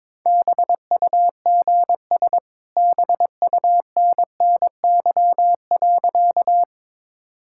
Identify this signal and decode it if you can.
Morse code, heard as a keyed tone, about 21 wpm, 690 Hz. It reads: BUGS BUNNY.
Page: morse-code-ninja-practice-22wpm